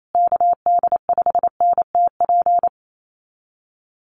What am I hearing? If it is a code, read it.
Morse code, heard as a keyed tone, about 28 wpm, 700 Hz. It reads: KD5NTP
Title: USA Call Signs Repeated 2x 28wpm